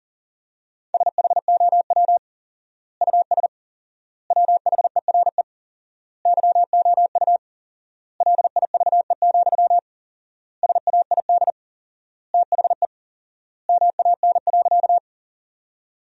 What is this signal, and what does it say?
Morse code, heard as a keyed tone, about 40 wpm, 700 Hz. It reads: SHOW US WHERE YOU LIVE, SAID THE MAN.
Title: 4x Word-Spacing - Sentences from Top 200 Words - 40wpm